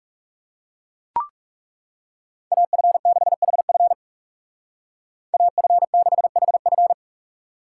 Morse code, heard as a keyed tone, about 39 wpm, 700 Hz. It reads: AF6HF AF6HF